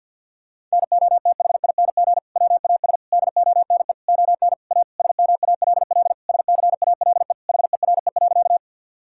Morse code, heard as a keyed tone, about 50 wpm, 700 Hz. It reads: NOTHING WAS DONE ON A SMALL SCALE HERE.
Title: Sentences from Top 1000 Words - Review Part 5 - 50wpm